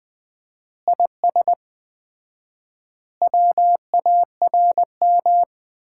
Morse code, heard as a keyed tone, about 21 wpm, 700 Hz. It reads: IS WARM
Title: Common QSO Phrases - 4x Word-Spacing - 20wpm